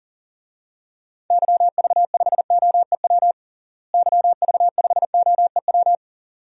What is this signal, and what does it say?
Morse code, heard as a keyed tone, about 40 wpm, 700 Hz. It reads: YV5OEW YV5OEW
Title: DXCC-100 Call Signs - 6 Characters - Repeated 2x - 40wpm